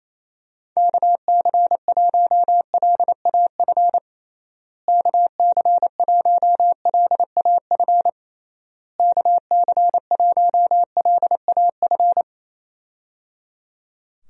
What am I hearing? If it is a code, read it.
Morse code, heard as a keyed tone, about 28 wpm, 700 Hz. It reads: KC1LAF KC1LAF KC1LAF
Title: USA Call Signs - 6 Characters - Repeated 3x - 28wpm